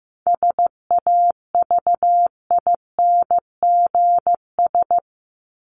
Morse code, heard as a keyed tone, about 15 wpm, 700 Hz. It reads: SAVINGS
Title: Top 1001-1250 Words 15wpm